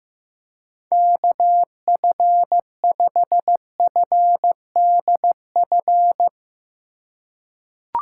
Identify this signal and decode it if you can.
Morse code, heard as a keyed tone, about 15 wpm, 700 Hz. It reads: KF5FDF
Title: USA Call Signs - 6 Characters - 4x Character Spacing - 15wpm